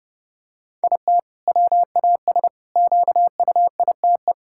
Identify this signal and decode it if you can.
Morse code, heard as a keyed tone, about 30 wpm, 700 Hz. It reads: IT WAS QUITE
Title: Sentences from Top 1000 Words - Review Part 5 - 30wpm